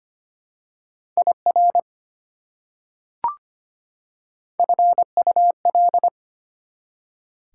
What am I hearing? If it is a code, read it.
Morse code, heard as a keyed tone, about 25 wpm, 700 Hz. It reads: IR FUL
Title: Prefixes and Suffixes 25wpm